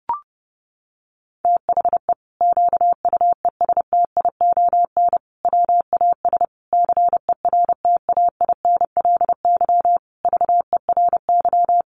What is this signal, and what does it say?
Morse code, heard as a keyed tone, about 30 wpm, 700 Hz. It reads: THE QUESTION WAS CERTAINLY VERY
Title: Sentences from Top 1200 Words 30wpm